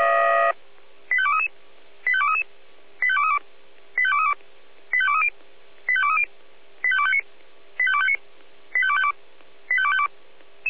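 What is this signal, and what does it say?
Five-tone selective call (ZVEI), heard as quick runs of five tones, tones of about 70 ms. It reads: 85233 85233 85232 85232 85230 85230 85259 85259 85252 85252
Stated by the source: ZVEI five-tone selective call received on SDR.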